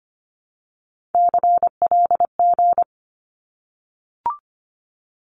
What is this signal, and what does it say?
Morse code, heard as a keyed tone, about 25 wpm, 700 Hz. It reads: CLG
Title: Abbreviations 25wpm